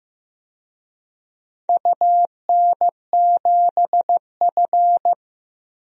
Morse code, heard as a keyed tone, about 16 wpm, 700 Hz. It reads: UN7F
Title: DXCC-100 Call Signs - 4 Characters - Repeated 2x - 15wpm